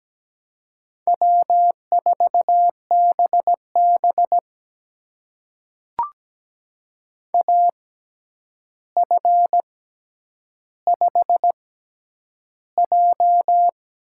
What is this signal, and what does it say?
Morse code, heard as a keyed tone, about 17 wpm, spaced as for 9 wpm, 700 Hz. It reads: W4BB AF5J
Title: USA Call Signs - 6x Character Spacing - 17wpm